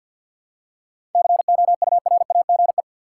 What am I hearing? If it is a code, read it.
Morse code, heard as a keyed tone, about 53 wpm, 700 Hz. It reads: COURAGE
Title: Seven Letter Words - 50wpm